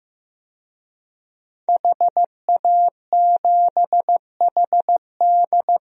Morse code, heard as a keyed tone, about 15 wpm, 700 Hz. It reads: HA7HD
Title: DXCC-100 Call Signs - 5 Characters - Repeated 3x - 15wpm